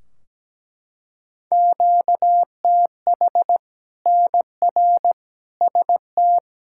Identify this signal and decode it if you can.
Morse code, heard as a keyed tone, about 17 wpm, 700 Hz. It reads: QTH NR ST
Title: Practice QSO - Exchange 1 - Beginning - 17wpm